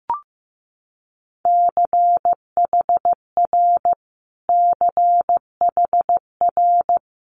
Morse code, heard as a keyed tone, about 15 wpm, 700 Hz. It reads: CHR CHR
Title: Multi-letter Phonemes Repeated 3x 15wpm